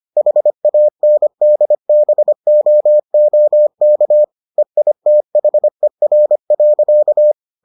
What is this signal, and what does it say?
Morse code, heard as a keyed tone, about 25 wpm, 600 Hz. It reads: HANDBOOK EITHER.